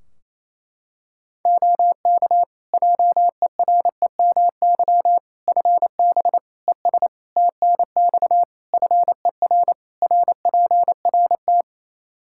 Morse code, heard as a keyed tone, about 28 wpm, 700 Hz. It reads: OK JEREMY FB ES TNX FER RPRT
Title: Practice QSO - Exchange 2 - Info - 28wpm